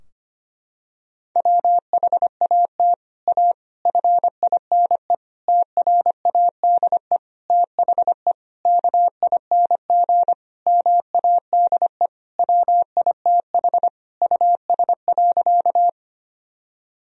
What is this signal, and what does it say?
Morse code, heard as a keyed tone, about 25 wpm, 700 Hz. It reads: WHAT A FINE TRADE THE KING MADE WITH US.
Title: Sentences from Top 1000 Words - Review Part 4 - 25wpm